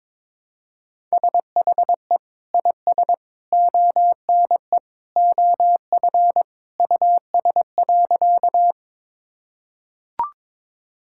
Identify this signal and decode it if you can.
Morse code, heard as a keyed tone, about 21 wpm, 700 Hz. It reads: SHE IS ONE OF US.
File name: practice / 22wpm / Sentences with 3-Word Phrases - 22wpm